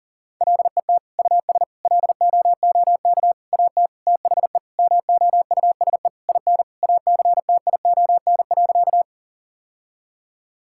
Morse code, heard as a keyed tone, about 40 wpm, 700 Hz. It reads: LET US LOOK AT THE MOUSE IN ACTION.
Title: Sentences from Top 1000 Words - Review Part 5 - 40wpm